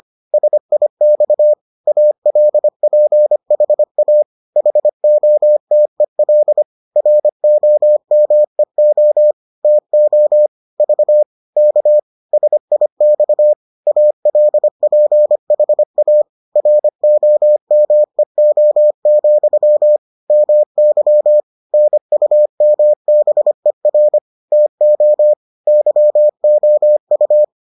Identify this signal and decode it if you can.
Morse code, heard as a keyed tone, about 25 wpm, 600 Hz. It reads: SIX ALPHA HOTEL ROMEO TO V K SIX ALPHA ROMEO, MY NUMBER TO YOU